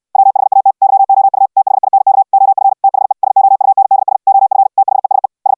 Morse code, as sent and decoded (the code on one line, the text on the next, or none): .-- .... .- - / -.-- --- ..- / - .... .. -. -.- / --- ..-. / - .... / ... --- ..- -. -.. ... / --- ..-. / - .... ... / ...
WHAT YOU THINK OF THE SOUNDS OF THESE S